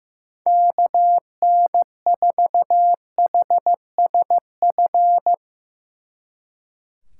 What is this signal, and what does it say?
Morse code, heard as a keyed tone, about 15 wpm, 700 Hz. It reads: KN4HSF
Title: Speed Racing - USA 2x3 Call Signs 15wpm